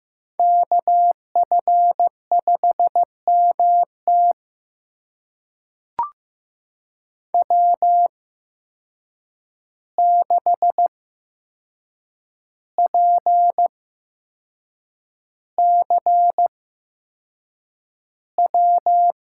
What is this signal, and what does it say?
Morse code, heard as a keyed tone, about 15 wpm, spaced as for 6 wpm, 700 Hz. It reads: KF5MT W6PCW
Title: USA Call Signs - 5 Characters - 8x Character Spacing - 15wpm